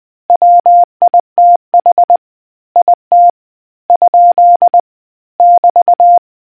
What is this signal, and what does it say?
Morse code, heard as a keyed tone, about 20 wpm, spaced as for 17 wpm, 700 Hz. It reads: WITH IT? <BT>